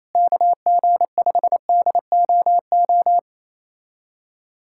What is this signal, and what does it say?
Morse code, heard as a keyed tone, about 28 wpm, 700 Hz. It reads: KG5DOO
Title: USA Callsign DE Callsign - 6 Characters - 28wpm